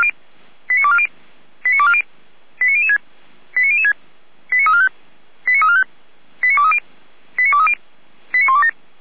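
Five-tone selective call (ZVEI), heard as quick runs of five tones, tones of about 70 ms. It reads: 89240 89240 89006 89006 89345 89345 89230 89230 89127
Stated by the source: ZVEI five-tone selective call 89143 received on SDR.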